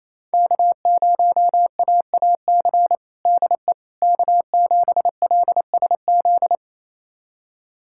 Morse code, heard as a keyed tone, about 27 wpm, 700 Hz. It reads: K0AAC DE K7LSZ
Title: USA Callsign DE Callsign - 5 Characters - 28wpm